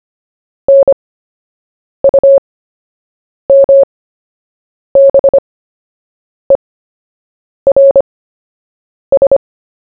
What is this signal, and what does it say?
Morse code, heard as a keyed tone, about 25 wpm, spaced as for 7 wpm, 550 Hz. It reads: NUMBERS
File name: cw / fists / Short-Story-1-7wpm